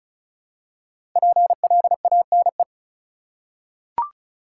Morse code, heard as a keyed tone, about 34 wpm, 700 Hz. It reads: PLANE
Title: Five Letter Words - 35wpm